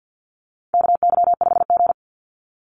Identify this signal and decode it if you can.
Morse code, heard as a keyed tone, about 50 wpm, 700 Hz. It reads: KY5G
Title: USA Call Signs 50wpm